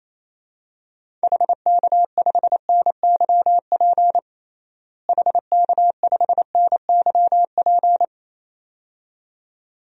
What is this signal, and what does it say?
Morse code, heard as a keyed tone, about 28 wpm, 700 Hz. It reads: HK5NYP HK5NYP
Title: DXCC-100 Call Signs - 6 Characters - Repeated 2x - 28wpm